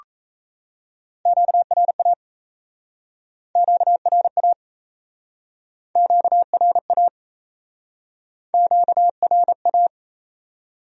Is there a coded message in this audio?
Yes: Morse code from 1.3 to 9.9 s, heard as a keyed tone, about 37 wpm, 700 Hz. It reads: QRA QRA QRA QRA